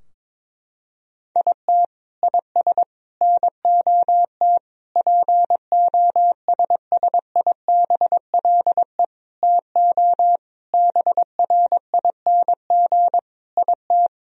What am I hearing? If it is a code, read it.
Morse code, heard as a keyed tone, about 22 wpm, 700 Hz. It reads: IT IS NOT POSSIBLE TO BRING IT
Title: Sentences from Top 1000 Words - Review Part 4 - 22wpm